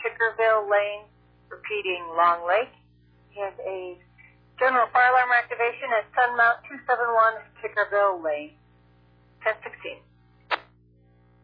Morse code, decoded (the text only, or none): none